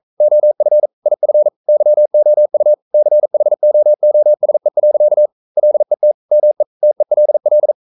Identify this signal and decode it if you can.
Morse code, heard as a keyed tone, about 42 wpm, 600 Hz. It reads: OF IF YOU CHOOSE. LET ME TELL
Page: Romanen Never af Ken Follet på engelsk.